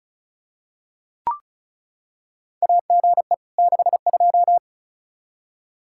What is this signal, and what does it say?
Morse code, heard as a keyed tone, about 36 wpm, 700 Hz. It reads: AGE 62
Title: K1USN SST QSO Practice 35wpm